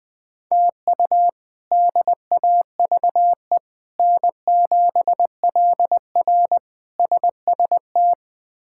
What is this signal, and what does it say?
Morse code, heard as a keyed tone, about 20 wpm, 700 Hz. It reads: TU DAVE N7LR SST
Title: K1USN SST QSO Practice - 20wpm